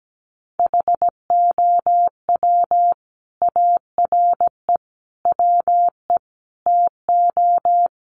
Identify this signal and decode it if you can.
Morse code, heard as a keyed tone, about 17 wpm, 700 Hz. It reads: HOW ARE WE TO
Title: Sentences from Top 1100 Words 17wpm